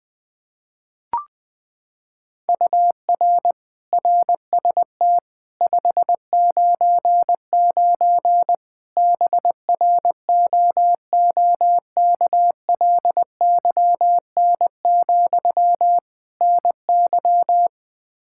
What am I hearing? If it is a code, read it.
Morse code, heard as a keyed tone, about 21 wpm, 700 Hz. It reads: UR RST 599 BROOKLYN, NY
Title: Practice QSO - Exchange 1 - Beginning - 20wpm